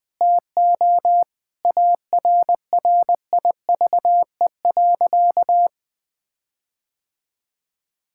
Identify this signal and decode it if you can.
Morse code, heard as a keyed tone, about 21 wpm, 700 Hz. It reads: TO ARRIVE.
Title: Sentences with 6-Word Phrases - 20wpm